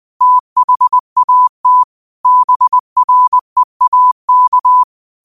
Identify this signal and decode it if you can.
Morse code, heard as a keyed tone, about 20 wpm, 1000 Hz. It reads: THAT BREAK
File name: qotdfiles / 20wpm